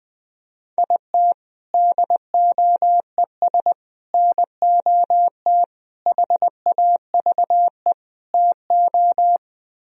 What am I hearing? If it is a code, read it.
Morse code, heard as a keyed tone, about 20 wpm, 700 Hz. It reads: IT DOES NOT HAVE TO
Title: Speed-Racing - Sets of 5 Words 17wpm